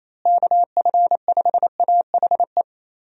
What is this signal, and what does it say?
Morse code, heard as a keyed tone, about 28 wpm, 700 Hz. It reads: KF5AHE
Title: USA Call Signs - 6 Characters - Repeated 2x - 28wpm